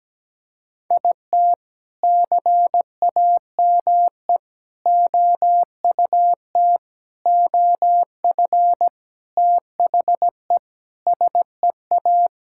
Morse code, heard as a keyed tone, about 17 wpm, 700 Hz. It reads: IT CAME OUT OF THE SEA
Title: Sentences from Top 1000 Words - Review Part 5 - 17wpm